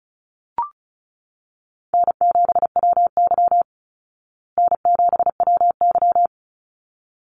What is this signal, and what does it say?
Morse code, heard as a keyed tone, about 34 wpm, 700 Hz. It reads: N7WY N7WY
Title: Call Signs Repeated 2x - 35wpm